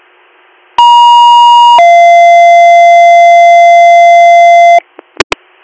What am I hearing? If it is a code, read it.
Two-tone page, heard as a one-second tone, then a three-second tone, about 950 then 690 Hz. It is heard from 0.8 to 4.8 s.